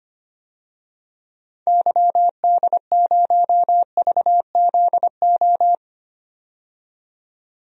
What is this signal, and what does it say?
Morse code, heard as a keyed tone, about 25 wpm, 700 Hz. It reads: YD0VZO
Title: DXCC-100 Call Signs - 6 Characters - 25wpm